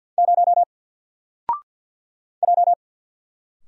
Morse code, heard as a keyed tone, about 52 wpm, 700 Hz. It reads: 0 J
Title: Single Letter-Number 50wpm